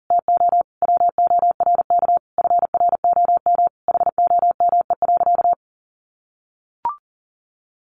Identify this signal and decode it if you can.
Morse code, heard as a keyed tone, about 40 wpm, 700 Hz. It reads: TO WORK FROM HOME.
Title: Sentences from Top 1500 Words 40wpm